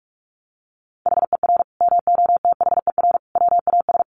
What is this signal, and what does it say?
Morse code, heard as a keyed tone, about 44 wpm, 700 Hz. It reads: HER MOTHER WAS
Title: Sentences from Top 1300 Words 45wpm